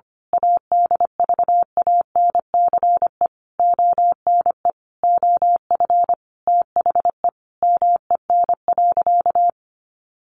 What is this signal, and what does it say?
Morse code, heard as a keyed tone, about 25 wpm, 700 Hz. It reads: ADVANCE ONE OF THE MEN.
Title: Sentences from Top 1400 Words 25wpm